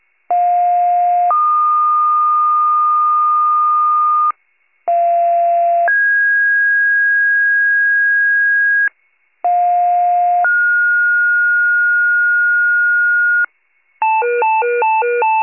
Two-tone page, heard as a one-second tone, then a three-second tone, about 710 then 1220 Hz, about 690 then 1650 Hz, about 710 then 1430 Hz, with noise. It is heard from 0.3 to 13.5 s.
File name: wcalert